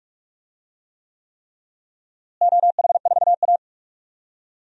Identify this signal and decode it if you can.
Morse code, heard as a keyed tone, about 42 wpm, 700 Hz. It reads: OH4A
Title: DXCC-100 Call Signs - 4 Characters - Repeated 3x - 45wpm